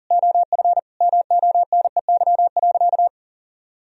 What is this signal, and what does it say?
Morse code, heard as a keyed tone, about 40 wpm, 700 Hz. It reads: OF MONEY.
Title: Sentences from Top 1000 Words - Review Part 3 - 40wpm